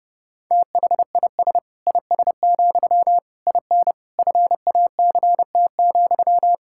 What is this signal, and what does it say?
Morse code, heard as a keyed tone, about 32 wpm, 700 Hz. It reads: THIS IS, IN FACT,
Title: Sentences from Top 1000 Words - Review Part 5 - 30wpm